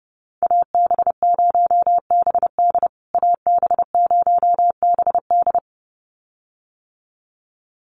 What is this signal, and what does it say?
Morse code, heard as a keyed tone, about 30 wpm, 700 Hz. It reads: AB0BD AB0BD
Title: USA Call Signs Repeated 2x 30wpm